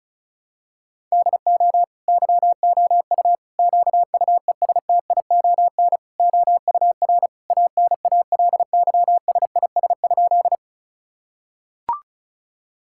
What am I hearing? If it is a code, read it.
Morse code, heard as a keyed tone, about 36 wpm, 700 Hz. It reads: DO YOU QUESTION OUR ANALYSIS?
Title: Sentences from Top 1000 Words - Review Part 5 - 35wpm